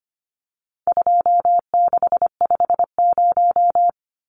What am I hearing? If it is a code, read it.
Morse code, heard as a keyed tone, about 25 wpm, 700 Hz. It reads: 2650
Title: Numbers - 4-Digit 25wpm